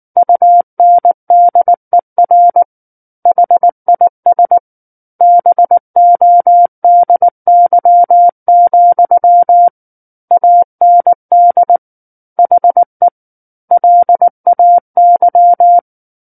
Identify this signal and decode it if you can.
Morse code, heard as a keyed tone, about 19 wpm, 700 Hz. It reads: UNDER HIS BODY, AND HE LAY
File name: Book One / 10-Chapter 10 - 19 WPM